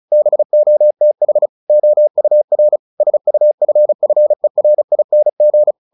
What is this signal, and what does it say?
Morse code, heard as a keyed tone, about 34 wpm, 600 Hz. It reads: BOTH OUR SUFFERING
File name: the old man and the sea 2 35wpm